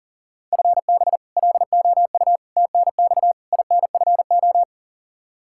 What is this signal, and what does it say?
Morse code, heard as a keyed tone, about 40 wpm, 700 Hz. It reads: FB LOU TNX INFO
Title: K1USN SST QSO Practice 40wpm